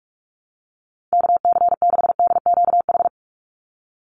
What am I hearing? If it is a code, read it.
Morse code, heard as a keyed tone, about 44 wpm, 700 Hz. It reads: KC6DQH
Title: USA Call Signs 45wpm